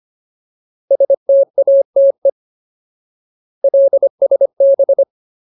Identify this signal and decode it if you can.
Morse code, heard as a keyed tone, about 25 wpm, 550 Hz. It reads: STATE LSB
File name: Ultimate Word List / 25WPM Ultimate